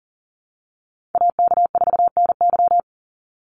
Morse code, heard as a keyed tone, about 40 wpm, 700 Hz. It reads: AK4NY
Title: Call Signs Repeated 2x - 40wpm